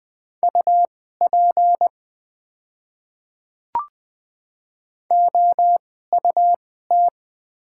Morse code, heard as a keyed tone, about 20 wpm, spaced as for 14 wpm, 700 Hz. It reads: UP OUT
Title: 2x Character Spacing - Top 300 Words - 20wpm